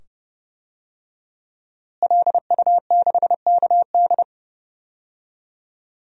Morse code, heard as a keyed tone, about 30 wpm, 700 Hz. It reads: LU6KD
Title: DXCC-100 Call Signs - 5 Characters - Repeated 2x - 30wpm